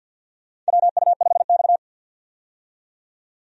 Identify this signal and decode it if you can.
Morse code, heard as a keyed tone, about 52 wpm, 700 Hz. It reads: WU5X
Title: USA Call Signs - 4 Characters - 50wpm